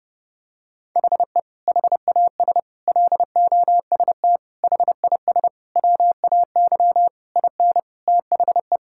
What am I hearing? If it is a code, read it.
Morse code, heard as a keyed tone, about 31 wpm, 700 Hz. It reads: HE HAS LOST HIS WAY IN THE
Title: Sentences from Top 1600 Words 30wpm